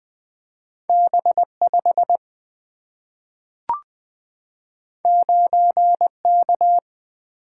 Morse code, heard as a keyed tone, about 20 wpm, 700 Hz. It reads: B 5 9 K
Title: Single-Letter Single-Number 20wpm